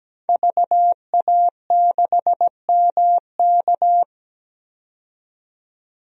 Morse code, heard as a keyed tone, about 17 wpm, 700 Hz. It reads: VA6MK
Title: DXCC-100 Call Signs - 5 Characters - 17wpm